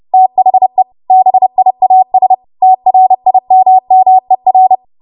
Morse code, heard as a keyed tone, about 30 wpm, 750 Hz. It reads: THE BIAS TRIMMER
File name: Archive / 30 WPM / 30WPM